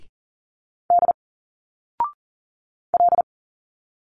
Morse code, heard as a keyed tone, about 40 wpm, 700 Hz. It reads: D L
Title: Single Letters 40wpm